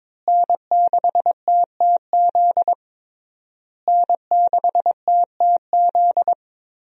Morse code, heard as a keyed tone, about 22 wpm, 700 Hz. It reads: N6TTZ N6TTZ
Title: USA Call Signs - 5 Characters - Repeated 3x - 22wpm